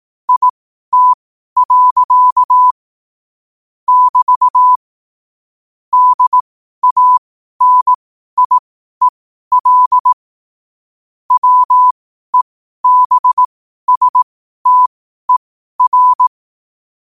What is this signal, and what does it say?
Morse code, heard as a keyed tone, about 18 wpm, spaced as for 12 wpm, 1000 Hz. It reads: IT. <BT> DANIEL WEBSTER
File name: qotdfiles / 12wpm